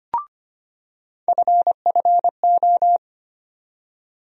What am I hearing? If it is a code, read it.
Morse code, heard as a keyed tone, about 25 wpm, 700 Hz. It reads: FFO
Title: USA Call Sign Postfixes - 3 Letters 25wpm